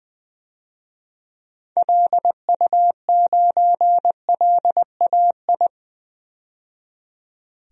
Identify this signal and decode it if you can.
Morse code, heard as a keyed tone, about 20 wpm, 700 Hz. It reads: LU9LAI
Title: DXCC-100 Call Signs - 6 Characters - 20wpm